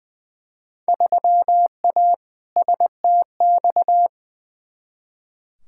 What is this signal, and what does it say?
Morse code, heard as a keyed tone, about 20 wpm, 700 Hz. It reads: 3A STX
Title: Field Day Exchanges - 20wpm